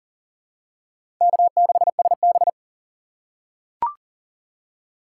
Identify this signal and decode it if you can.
Morse code, heard as a keyed tone, about 40 wpm, 700 Hz. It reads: K6SB
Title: USA Call Signs - 4 Characters - 4x Character Spacing - 40wpm